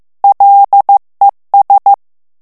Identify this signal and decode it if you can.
Morse code, heard as a keyed tone, about 15 wpm, 800 Hz. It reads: LES
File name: FRANCAIS TEXTE CLAIR 2 VITESSE 14 MOTS MINUTES